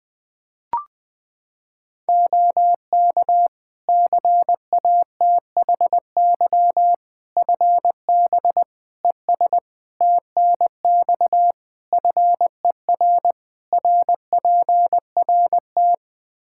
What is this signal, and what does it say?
Morse code, heard as a keyed tone, about 20 wpm, 700 Hz. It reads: OK CATHY FB ES TNX FER RPRT
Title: Practice QSO - Exchange 2 - Info - 20wpm